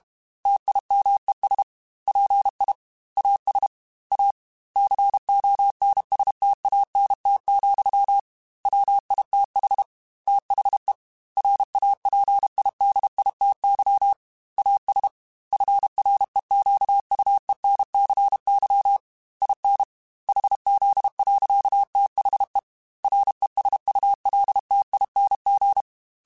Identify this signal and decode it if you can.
Morse code, heard as a keyed tone, about 32 wpm, 800 Hz. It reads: TIMES PI AS A CONSTANT, WITH THE RAPIDITY AS FREQUENCY IN HZ. THE RESULTING
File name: Technical Subjects / Technical Subjects-mp3s / XC @30